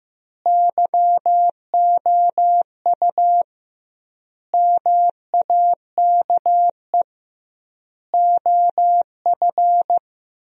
Morse code, heard as a keyed tone, about 15 wpm, 700 Hz. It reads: YOU MAKE OF
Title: 2x Word-Spacing - Sentences from Top 100 Words - 15wpm